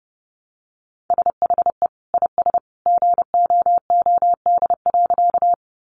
Morse code, heard as a keyed tone, about 30 wpm, 700 Hz. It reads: SHE IS GOOD.
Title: Easy Phrases 30wpm